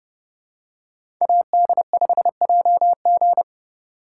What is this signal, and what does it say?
Morse code, heard as a keyed tone, about 28 wpm, 700 Hz. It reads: AD5JG